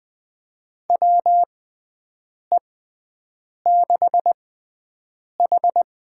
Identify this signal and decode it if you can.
Morse code, heard as a keyed tone, about 20 wpm, spaced as for 7 wpm, 700 Hz. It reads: WE6H
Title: USA Call Signs - 4 Characters - 6x Character Spacing - 20wpm